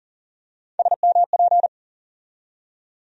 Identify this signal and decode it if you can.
Morse code, heard as a keyed tone, about 40 wpm, 700 Hz. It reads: SMP
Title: USA Call Sign Postfixes - 3 Letters 40wpm